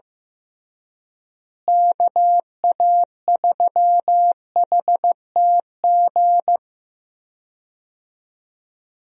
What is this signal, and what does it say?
Morse code, heard as a keyed tone, about 15 wpm, 700 Hz. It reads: KA3HTG
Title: USA Call Signs - 6 Characters - Repeated 3x - 15wpm